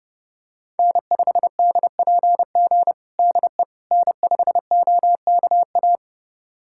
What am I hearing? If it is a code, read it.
Morse code, heard as a keyed tone, about 30 wpm, 700 Hz. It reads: N5DPG DE N5OKA
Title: USA Callsign DE Callsign - 5 Characters - 30wpm